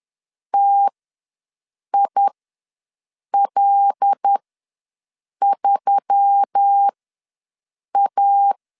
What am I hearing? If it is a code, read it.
Morse code, heard as a keyed tone, about 10 wpm, spaced as for 6 wpm, 800 Hz. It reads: TIL3A
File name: LEZIONI / 5 LBVC / 5 LBVC ws 10wpm 3733